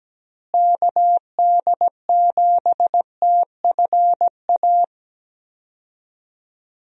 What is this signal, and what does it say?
Morse code, heard as a keyed tone, about 17 wpm, 700 Hz. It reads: KD7TFA
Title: USA Call Signs - 6 Characters - Repeated 3x - 17wpm